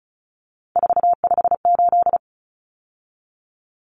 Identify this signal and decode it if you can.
Morse code, heard as a keyed tone, about 34 wpm, 700 Hz. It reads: 458
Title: Common Cut Numbers - 3-Digit 35wpm